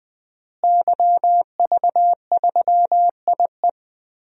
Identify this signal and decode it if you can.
Morse code, heard as a keyed tone, about 20 wpm, 700 Hz. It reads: YV3IE
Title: DXCC-100 Call Signs - 5 Characters - Repeated 2x - 20wpm